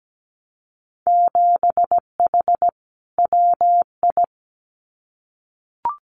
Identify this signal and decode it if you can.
Morse code, heard as a keyed tone, about 17 wpm, 700 Hz. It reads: 7H WI
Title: Winter Field Day Exchanges Encore 17wpm